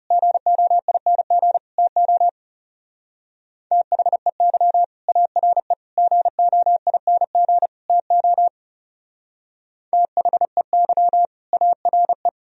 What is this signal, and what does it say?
Morse code, heard as a keyed tone, about 38 wpm, 700 Hz. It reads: GOING TO THEY ARE GOING TO THEY ARE